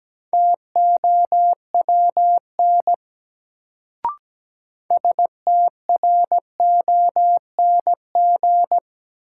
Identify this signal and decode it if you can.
Morse code, heard as a keyed tone, about 17 wpm, 700 Hz. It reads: TOWN STRONG
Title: Sentences from Top 500 Words - New Word Drill - 17wpm